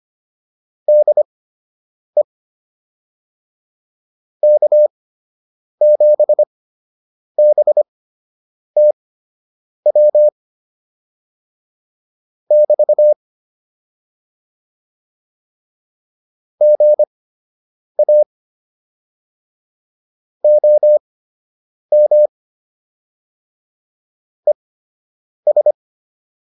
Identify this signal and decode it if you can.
Morse code, heard as a keyed tone, about 25 wpm, spaced as for 8 wpm, 600 Hz. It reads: DE K7BTW <BT> GA OM ES